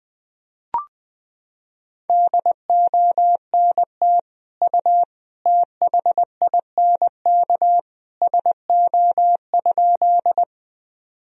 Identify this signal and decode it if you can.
Morse code, heard as a keyed tone, about 21 wpm, 700 Hz. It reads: DONT U THINK SO?
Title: Sentences from Top 300 Words and Abbreviations 20wpm